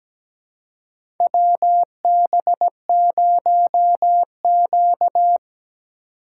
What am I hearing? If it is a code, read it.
Morse code, heard as a keyed tone, about 17 wpm, 700 Hz. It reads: WB0Q
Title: USA Call Signs - 4x Character Spacing - 17wpm